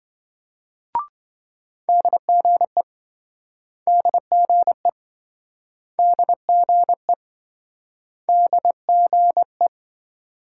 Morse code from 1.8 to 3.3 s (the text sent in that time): DGE